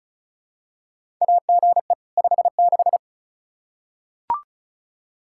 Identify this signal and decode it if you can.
Morse code, heard as a keyed tone, about 36 wpm, 700 Hz. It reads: AGE 56
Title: K1USN SST QSO Practice 35wpm